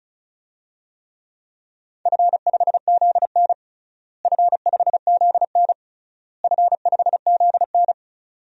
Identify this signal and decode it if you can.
Morse code, heard as a keyed tone, about 34 wpm, 700 Hz. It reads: F5ZN F5ZN F5ZN
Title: DXCC-100 Call Signs - 4 Characters - Repeated 3x - 35wpm